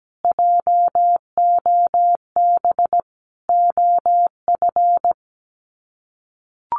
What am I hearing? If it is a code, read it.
Morse code, heard as a keyed tone, about 17 wpm, 700 Hz. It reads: JOB OF
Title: Sets of 5 Words Encore2 17wpm